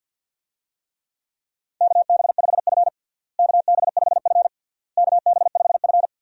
Morse code, heard as a keyed tone, about 53 wpm, 700 Hz. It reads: KB5F KB5F KB5F
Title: USA Call Signs - 4 Characters - Repeated 3x - 50wpm